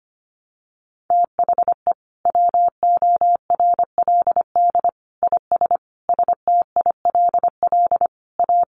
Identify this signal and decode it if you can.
Morse code, heard as a keyed tone, about 25 wpm, 700 Hz. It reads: THE WORLD IS STILL A
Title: Sentences from Top 1200 Words 25wpm